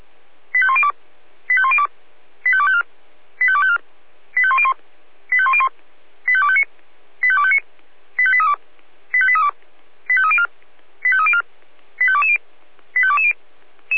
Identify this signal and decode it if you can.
Five-tone selective call (ZVEI), heard as quick runs of five tones, tones of about 70 ms. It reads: 85102 85102 85254 85254 85101 85101 85259 85259 85832 85832 85304 85304 85220 85220